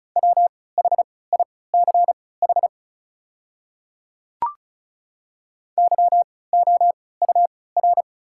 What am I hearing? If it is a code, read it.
Morse code, heard as a keyed tone, about 35 wpm, spaced as for 20 wpm, 700 Hz. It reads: WHICH YOUR
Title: 3x Character Spacing - Top 200 Words - 35wpm